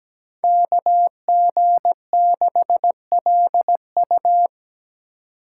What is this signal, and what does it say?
Morse code, heard as a keyed tone, about 17 wpm, 700 Hz. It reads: KG6LU
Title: USA Call Signs - 5 Characters - 3x Character Spacing - 17wpm